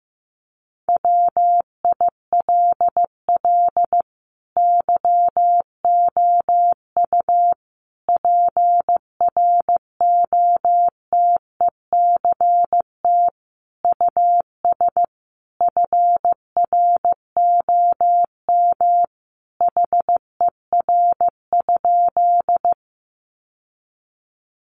Morse code, heard as a keyed tone, about 15 wpm, 700 Hz. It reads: WILL YOU PROTECT US FROM HER?
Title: Sentences from Top 900 Words 15wpm